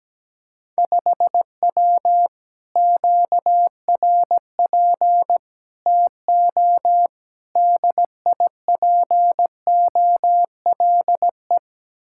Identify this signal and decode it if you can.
Morse code, heard as a keyed tone, about 17 wpm, 700 Hz. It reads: 5W QRP TO DIPOLE
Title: Speed-Racing Common QSO Phrases - 17wpm